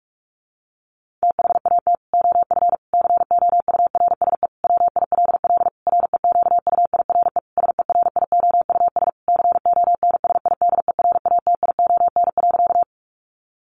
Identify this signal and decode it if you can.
Morse code, heard as a keyed tone, about 44 wpm, 700 Hz. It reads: THAT OF COURSE WILL REQUIRE SERIOUS CONSIDERATION.
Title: Sentences from Top 1500 Words 45wpm